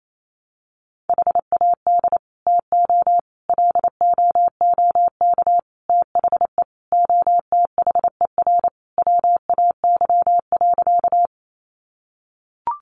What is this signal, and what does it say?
Morse code, heard as a keyed tone, about 28 wpm, 700 Hz. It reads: HAD TO LOOK THE OTHER WAY.
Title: Sentences from Top 100 Words 28wpm